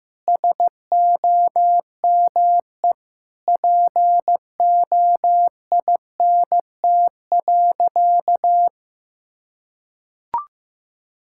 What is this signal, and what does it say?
Morse code, heard as a keyed tone, about 16 wpm, 700 Hz. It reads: SOME POINT.
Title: Sentences from Top 1000 Words - Review Part 5 - 15wpm